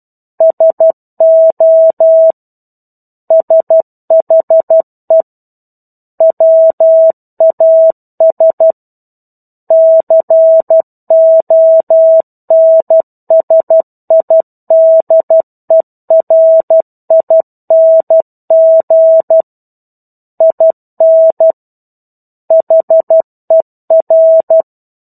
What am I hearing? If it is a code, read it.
Morse code, heard as a keyed tone, about 12 wpm, 650 Hz. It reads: SO SHE WAS CONSIDERING IN HER
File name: 01-Chapter-01 12 WPM